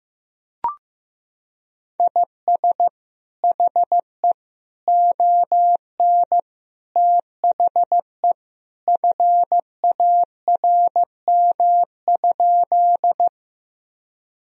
Morse code, heard as a keyed tone, about 15 wpm, 700 Hz. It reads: IS HE ON THE FARM?
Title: Sentences from Top 1000 Words - Review Part 2 - 15wpm